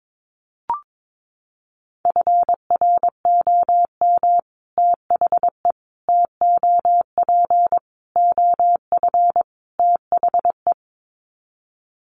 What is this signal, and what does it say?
Morse code, heard as a keyed tone, about 22 wpm, 700 Hz. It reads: FROM THE TOP OF THE
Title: Sets of 5 Words Encore 22wpm